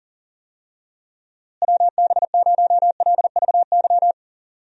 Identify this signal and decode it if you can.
Morse code, heard as a keyed tone, about 40 wpm, 700 Hz. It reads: WB0LVY
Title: USA Call Signs - 6 Characters - Repeated 2x - 40wpm